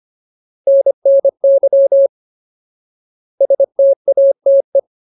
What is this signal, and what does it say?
Morse code, heard as a keyed tone, about 25 wpm, 550 Hz. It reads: NNY STATE
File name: Ultimate Word List / 25WPM Ultimate